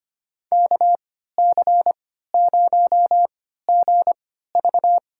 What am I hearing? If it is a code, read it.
Morse code, heard as a keyed tone, about 25 wpm, spaced as for 14 wpm, 700 Hz. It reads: KC0GV
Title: USA Call Signs - 3x Character Spacing - 25wpm